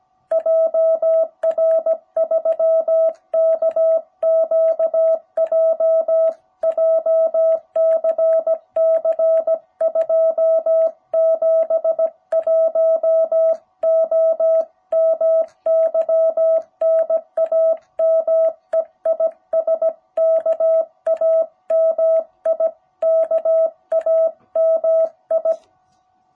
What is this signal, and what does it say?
Morse code, heard as a keyed tone, about 17 wpm, 660 Hz. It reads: JR3KQJ JCC 271O MY NAME IS KAMI KAMI
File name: jh8cbh / 2021cw